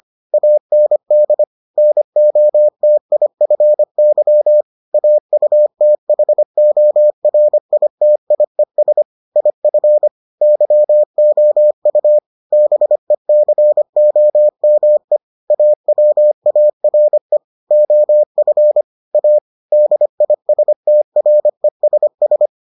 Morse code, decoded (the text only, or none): AND NOTIFY AUTHORITIES IF YOU BECOME AWARE OF A DISTRESS